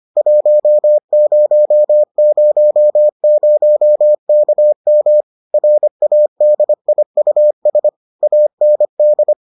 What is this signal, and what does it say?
Morse code, heard as a keyed tone, about 25 wpm, 600 Hz. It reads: 1000KM RADIUS AND